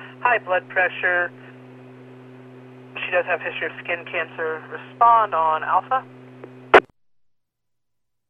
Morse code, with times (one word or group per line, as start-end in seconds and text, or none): none